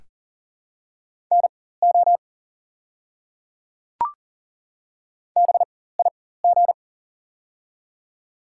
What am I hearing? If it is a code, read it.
Morse code, heard as a keyed tone, about 40 wpm, spaced as for 19 wpm, 700 Hz. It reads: NO BIG
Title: 4x Character Spacing - Top 200 Words - 40wpm